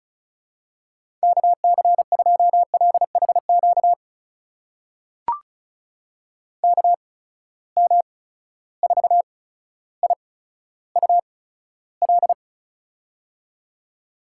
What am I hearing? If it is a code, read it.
Morse code, heard as a keyed tone, about 36 wpm, spaced as for 15 wpm, 700 Hz. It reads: KC2LHQ KM4IUL
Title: USA Call Signs - 6 Characters - 8x Character Spacing - 35wpm